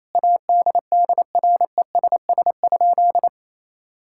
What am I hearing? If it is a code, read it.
Morse code, heard as a keyed tone, about 28 wpm, 700 Hz. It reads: ADDRESS?
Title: Sentences from Top 1000 Words - Review Part 6 - 28wpm